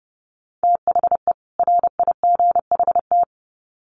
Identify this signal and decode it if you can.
Morse code, heard as a keyed tone, about 30 wpm, 700 Hz. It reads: THE RIGHT
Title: Sets of 2 Words 30wpm